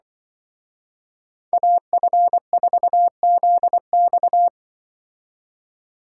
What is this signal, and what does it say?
Morse code, heard as a keyed tone, about 24 wpm, 700 Hz. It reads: AF4ZX
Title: Speed Racing - USA 2x2 Call Signs 20wpm